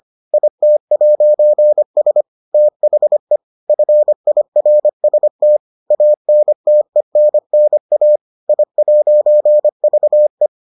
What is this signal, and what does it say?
Morse code, heard as a keyed tone, about 25 wpm, 600 Hz. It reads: IT'S THE FIRST ANTENNA I'VE